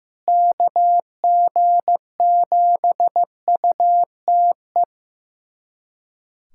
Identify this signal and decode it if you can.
Morse code, heard as a keyed tone, about 15 wpm, 700 Hz. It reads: KG7UTE
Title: USA Callsign DE Callsign - 6 Characters - 15wpm